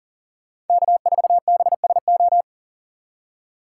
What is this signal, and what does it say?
Morse code, heard as a keyed tone, about 49 wpm, 700 Hz. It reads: K4BSO
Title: Speed Racing - USA 1x3 Call Signs 40wpm